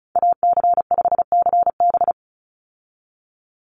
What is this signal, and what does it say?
Morse code, heard as a keyed tone, about 34 wpm, 700 Hz. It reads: AC5CB
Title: USA Call Signs 35wpm